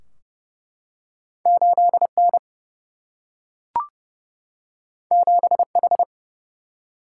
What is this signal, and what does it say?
Morse code, heard as a keyed tone, about 28 wpm, 700 Hz. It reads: 8 N 7 H
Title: Single-Letter Single-Number 30wpm